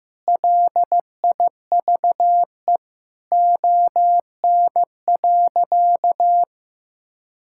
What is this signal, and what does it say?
Morse code, heard as a keyed tone, about 15 wpm, 700 Hz. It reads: LIVE ON.
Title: Sentences from Top 1000 Words - Review Part 5 - 15wpm